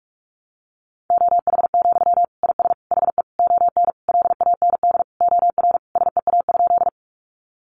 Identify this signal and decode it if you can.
Morse code, heard as a keyed tone, about 44 wpm, 700 Hz. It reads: OH, IS HE ON LAND OR SEA?
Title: Sentences from Top 500 Words 45wpm